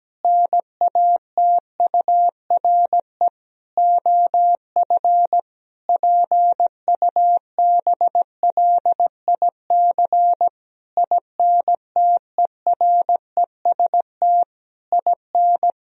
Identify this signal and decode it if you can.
Morse code, heard as a keyed tone, about 17 wpm, 700 Hz. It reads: NATURE OF PUBLIC INTEREST IN
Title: Sentences from Top 1000 Words - Review Part 3 - 17wpm